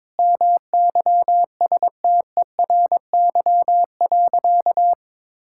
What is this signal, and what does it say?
Morse code, heard as a keyed tone, about 21 wpm, 700 Hz. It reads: MYSTERY.
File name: Sentences with 7-Word Phrases - 22wpm